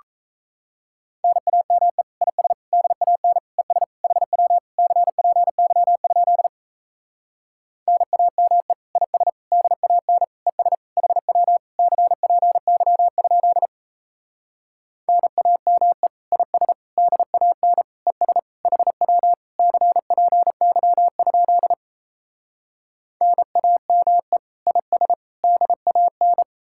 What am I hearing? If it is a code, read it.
Morse code, heard as a keyed tone, about 37 wpm, 700 Hz. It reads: NAME IS DAN ES HW CPY? NAME IS DAN ES HW CPY? NAME IS DAN ES HW CPY? NAME IS DAN